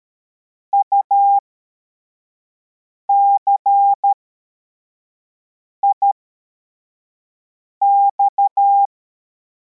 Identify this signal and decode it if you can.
Morse code, heard as a keyed tone, about 13 wpm, spaced as for 4 wpm, 800 Hz. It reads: UCIX
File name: formation / cw / Text42